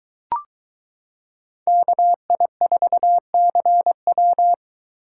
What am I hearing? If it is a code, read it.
Morse code, heard as a keyed tone, about 23 wpm, 700 Hz. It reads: KI4CW